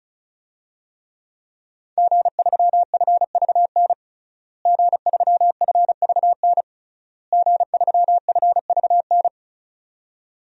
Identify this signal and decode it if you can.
Morse code, heard as a keyed tone, about 34 wpm, 700 Hz. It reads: G3FVN G3FVN G3FVN
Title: DXCC-100 Call Signs - 5 Characters - Repeated 3x - 35wpm